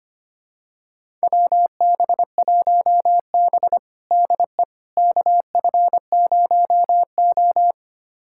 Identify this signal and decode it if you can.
Morse code, heard as a keyed tone, about 25 wpm, 700 Hz. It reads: WB1B DE KF0O
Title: USA Callsign DE Callsign - 4 Characters - 25wpm